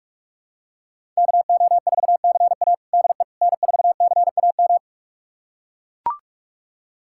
Morse code, heard as a keyed tone, about 43 wpm, 700 Hz. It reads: KO4CA DE N4CAM
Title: USA Callsign DE Callsign - 5 Characters - 45wpm